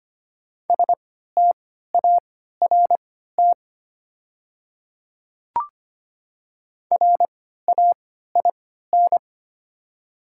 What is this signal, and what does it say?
Morse code, heard as a keyed tone, about 25 wpm, spaced as for 14 wpm, 700 Hz. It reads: START RAIN
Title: 3x Character Spacing - Top 300 Words - 25wpm